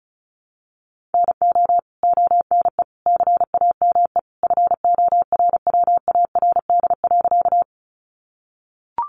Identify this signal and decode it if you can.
Morse code, heard as a keyed tone, about 34 wpm, 700 Hz. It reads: NO ONE CAME FORWARD.
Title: Sentences from Top 1300 Words 35wpm